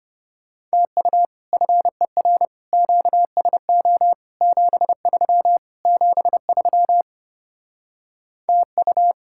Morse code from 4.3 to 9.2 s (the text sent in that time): 73 73 TU